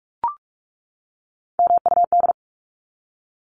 Morse code, heard as a keyed tone, about 44 wpm, 700 Hz. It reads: MUD
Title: Top 1751-2000 Words 45wpm